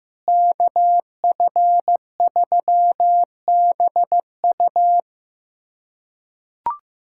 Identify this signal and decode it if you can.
Morse code, heard as a keyed tone, about 15 wpm, 700 Hz. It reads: KF3BU
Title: USA Callsign DE Callsign - 5 Characters - 15wpm